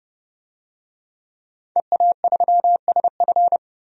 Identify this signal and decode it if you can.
Morse code, heard as a keyed tone, about 30 wpm, 700 Hz. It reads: EA3SF
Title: DXCC-100 Call Signs - 5 Characters - 30wpm